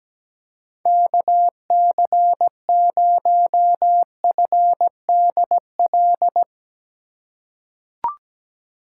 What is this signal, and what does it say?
Morse code, heard as a keyed tone, about 17 wpm, 700 Hz. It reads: KC0FDL
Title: USA Call Signs - 6 Characters - 4x Character Spacing - 17wpm